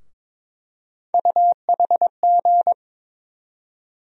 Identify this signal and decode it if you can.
Morse code, heard as a keyed tone, about 21 wpm, 700 Hz. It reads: UHG
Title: USA Call Sign Postfixes - 3 Letters 22wpm